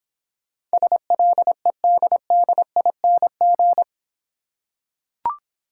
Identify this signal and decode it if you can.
Morse code, heard as a keyed tone, about 26 wpm, 700 Hz. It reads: SLEDDING